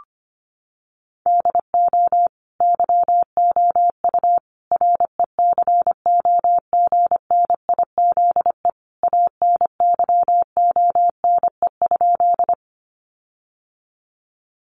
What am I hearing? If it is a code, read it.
Morse code, heard as a keyed tone, about 25 wpm, 700 Hz. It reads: DO YOU RECOGNIZE ANYONE?
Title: Sentences from Top 1400 Words 25wpm